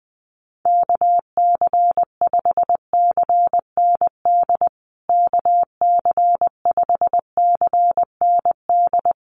Morse code, heard as a keyed tone, about 20 wpm, 700 Hz. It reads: KC5CND KC5CND
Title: USA Call Signs Repeated 2x 20wpm